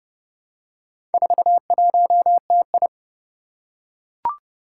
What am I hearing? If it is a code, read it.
Morse code, heard as a keyed tone, about 29 wpm, 690 Hz. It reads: 41TI
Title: Four Challenging Letter-Number Combinations 30wpm